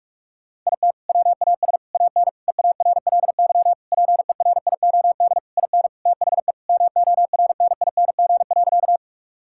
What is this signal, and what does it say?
Morse code, heard as a keyed tone, about 41 wpm, 700 Hz. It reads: IT WAS AN EARLY PERIOD IN THE MORNING.
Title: Sentences from Top 1000 Words - Review Part 5 - 45wpm